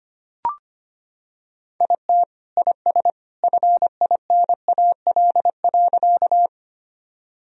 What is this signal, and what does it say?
Morse code, heard as a keyed tone, about 25 wpm, 700 Hz. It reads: IT IS FINAL.
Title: Sentences from Top 1000 Words - Review Part 2 - 25wpm